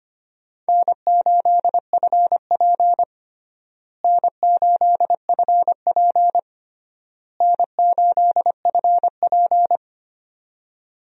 Morse code, heard as a keyed tone, about 25 wpm, 700 Hz. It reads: N8FP N8FP N8FP
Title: USA Call Signs - 4 Characters - Repeated 3x - 25wpm